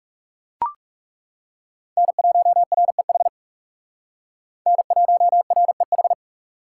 Morse code, heard as a keyed tone, about 37 wpm, 700 Hz. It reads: N1REH N1REH
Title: Speed Racing - USA 1x3 Call Signs 30wpm